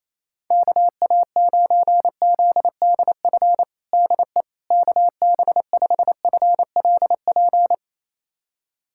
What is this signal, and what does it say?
Morse code, heard as a keyed tone, about 28 wpm, 700 Hz. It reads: KA9ZDF DE KB5FLP
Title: USA Callsign DE Callsign - 6 Characters - 28wpm